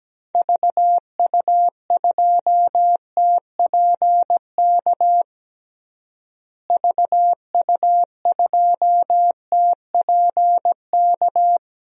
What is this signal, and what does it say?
Morse code, heard as a keyed tone, about 17 wpm, 700 Hz. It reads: VU2TPK VU2TPK
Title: DXCC-100 Call Signs - 6 Characters - Repeated 3x - 17wpm